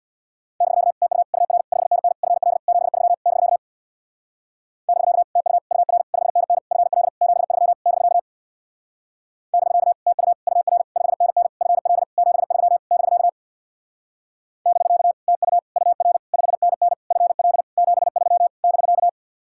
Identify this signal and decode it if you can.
Morse code, heard as a keyed tone, about 64 wpm, 700 Hz. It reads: <BK> TU UR 5NN FL 73 <BK> <BK> TU UR 5NN FL 73 <BK> <BK> TU UR 5NN FL 73 <BK> <BK> TU UR 5NN FL 73 <BK>